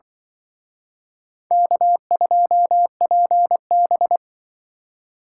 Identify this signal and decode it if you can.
Morse code, heard as a keyed tone, about 23 wpm, 700 Hz. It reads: K2PB
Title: Speed Racing - USA Call Signs 20wpm